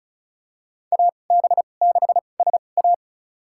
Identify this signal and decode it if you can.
Morse code, heard as a keyed tone, about 36 wpm, spaced as for 25 wpm, 700 Hz. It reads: AB6SA
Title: USA Call Signs - 5 Characters - 2x Character Spacing - 35wpm